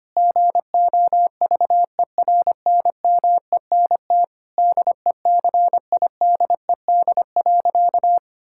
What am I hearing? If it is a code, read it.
Morse code, heard as a keyed tone, about 25 wpm, 700 Hz. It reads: GOVERNMENT DECIDED.
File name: practice / 25wpm / Sentences with 3-Word Phrases - 25wpm